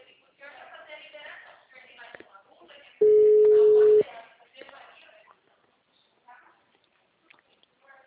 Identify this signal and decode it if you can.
Ringback tone (Europe, 425 Hz), with speech.
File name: out- -2010- - - .14058